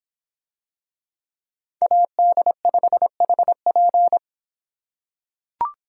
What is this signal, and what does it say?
Morse code, heard as a keyed tone, about 25 wpm, 700 Hz. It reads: AD5HP